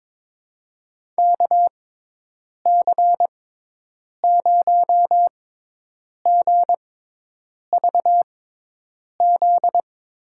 Morse code, heard as a keyed tone, about 21 wpm, spaced as for 8 wpm, 700 Hz. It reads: KC0GVZ
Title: USA Call Signs - 6 Characters - 6x Character Spacing - 22wpm